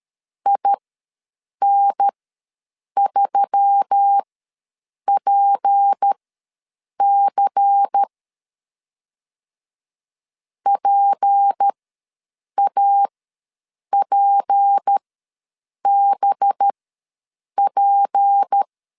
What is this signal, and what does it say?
Morse code, heard as a keyed tone, about 13 wpm, spaced as for 7 wpm, 800 Hz. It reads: IN3PC PAPBP
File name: LEZIONI / 6 PKX0 / 6 P ws 12wpm 3117